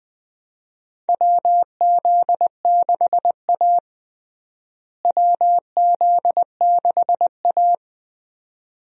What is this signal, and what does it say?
Morse code, heard as a keyed tone, about 20 wpm, 700 Hz. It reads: WZ6A WZ6A
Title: USA Call Signs - 4 Characters - Repeated 2x - 20wpm